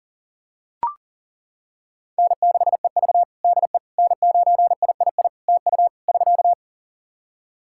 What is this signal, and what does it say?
Morse code, heard as a keyed tone, about 40 wpm, 700 Hz. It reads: N6EV DE N9III TU <SK>
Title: Practice QSO - Exchange 4 - Ending - 40wpm